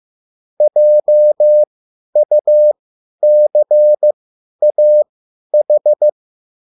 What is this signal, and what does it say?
Morse code, heard as a keyed tone, about 15 wpm, spaced as for 10 wpm, 600 Hz. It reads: JUCAH
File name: QSL / CW lesson